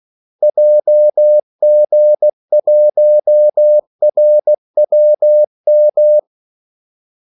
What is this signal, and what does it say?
Morse code, heard as a keyed tone, about 16 wpm, 600 Hz. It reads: JG1RWM
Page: Kaldesignaler 16 wpm